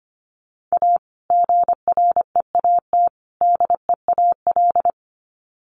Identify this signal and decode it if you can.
Morse code, heard as a keyed tone, about 25 wpm, 700 Hz. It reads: A GREAT DEAL
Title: Sets of 3 Words 25wpm